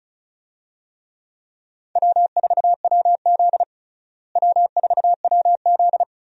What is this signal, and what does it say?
Morse code, heard as a keyed tone, about 36 wpm, 700 Hz. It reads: W4WZ W4WZ
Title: USA Call Signs - 4 Characters - Repeated 3x - 35wpm